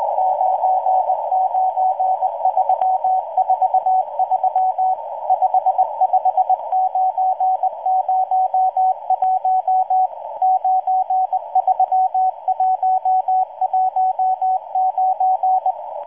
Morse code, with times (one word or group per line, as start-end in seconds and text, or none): 0.2-4.9 99343
5.3-10.1 55901
10.4-15.7 93119